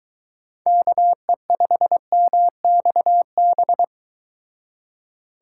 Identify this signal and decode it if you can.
Morse code, heard as a keyed tone, about 23 wpm, 700 Hz. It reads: KE5MXB